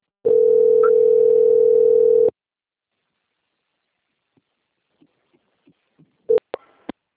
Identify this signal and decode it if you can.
Ringback tone (North America), 440 + 480 Hz.